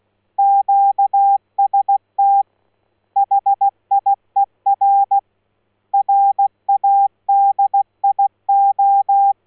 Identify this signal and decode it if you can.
Morse code, heard as a keyed tone, about 16 wpm, 800 Hz. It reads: QST HIER RADIO
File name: Morse-Hier-Radio-Kootwijk